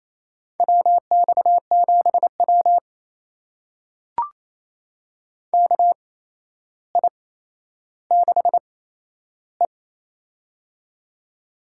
Morse code, heard as a keyed tone, about 28 wpm, spaced as for 12 wpm, 700 Hz. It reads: WX7W KI6E
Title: USA Call Signs - 4 Characters - 8x Character Spacing - 28wpm